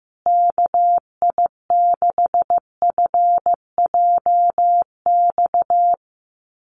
Morse code, heard as a keyed tone, about 15 wpm, 700 Hz. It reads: KI6FJX
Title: USA Call Signs Repeated 3x 15wpm